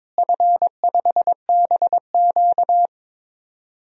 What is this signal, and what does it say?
Morse code, heard as a keyed tone, about 22 wpm, 700 Hz. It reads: F5BQ
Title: DXCC-100 Call Signs - 4 Characters - Repeated 2x - 22wpm